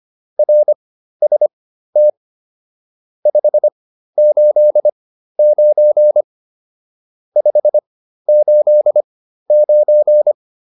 Morse code, heard as a keyed tone, about 25 wpm, spaced as for 13 wpm, 600 Hz. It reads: RST 589 589